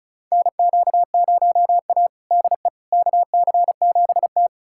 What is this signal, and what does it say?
Morse code, heard as a keyed tone, about 36 wpm, 700 Hz. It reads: NQ0A DE KC7T
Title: USA Callsign DE Callsign - 4 Characters - 35wpm